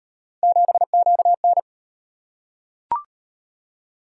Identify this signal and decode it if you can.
Morse code, heard as a keyed tone, about 36 wpm, 700 Hz. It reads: 7QN